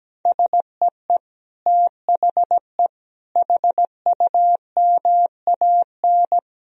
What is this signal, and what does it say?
Morse code, heard as a keyed tone, about 17 wpm, 700 Hz. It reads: SEE THE HUMAN
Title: Sentences from Top 1000 Words - Review Part 4 - 17wpm